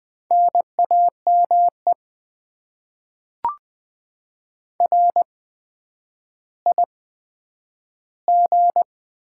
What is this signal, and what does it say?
Morse code, heard as a keyed tone, about 20 wpm, spaced as for 10 wpm, 700 Hz. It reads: NAME RIG
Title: Top 20 QSO Elements - 8x Character Spacing - 20wpm